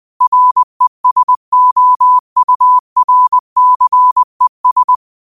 Morse code, heard as a keyed tone, about 20 wpm, 1000 Hz. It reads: RESOURCES